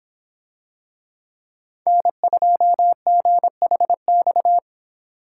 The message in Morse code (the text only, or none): N2GHX